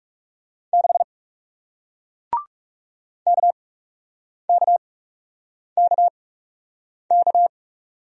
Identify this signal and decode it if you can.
Morse code, heard as a keyed tone, about 41 wpm, spaced as for 8 wpm, 700 Hz. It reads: 6 KKKK